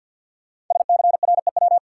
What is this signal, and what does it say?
Morse code, heard as a keyed tone, about 52 wpm, 700 Hz. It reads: SCREW
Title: Sentences from Top 1600 Words - New Word Drill Alt - 50wpm